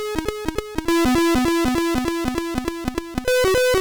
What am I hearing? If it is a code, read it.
There is no code.